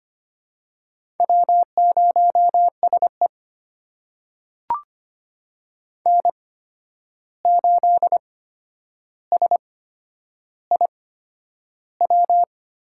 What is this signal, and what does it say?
Morse code, heard as a keyed tone, about 25 wpm, spaced as for 10 wpm, 700 Hz. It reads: W0SE N8SIW
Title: USA Call Signs - 8x Character Spacing - 25wpm